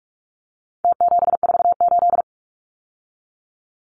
Morse code, heard as a keyed tone, about 44 wpm, 700 Hz. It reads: T748
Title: Numbers - Contest Style - 4-Digit 45wpm